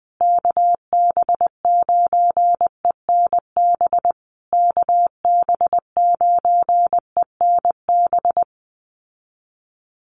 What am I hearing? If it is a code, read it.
Morse code, heard as a keyed tone, about 19 wpm, 700 Hz. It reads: KB9ENB KB9ENB
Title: USA Call Signs Repeated 3x 20wpm